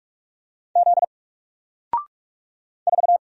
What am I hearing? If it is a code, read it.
Morse code, heard as a keyed tone, about 46 wpm, 700 Hz. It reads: Z 4
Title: Single Letter-Number Alt 45wpm